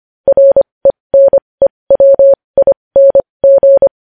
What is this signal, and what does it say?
Morse code, heard as a keyed tone, about 25 wpm, spaced as for 20 wpm, 550 Hz. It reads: RENEWING
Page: News Headlines in Morse Code at 20 WPM